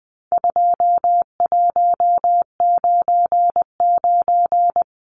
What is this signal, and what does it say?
Morse code, heard as a keyed tone, about 20 wpm, 700 Hz. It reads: 2199
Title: Numbers - 4-Digit 20wpm